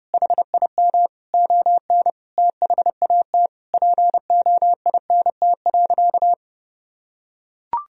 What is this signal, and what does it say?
Morse code, heard as a keyed tone, about 30 wpm, 700 Hz. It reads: HIM ON THAT POINT.
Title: Sentences from Top 1000 Words - Review Part 1 - 30wpm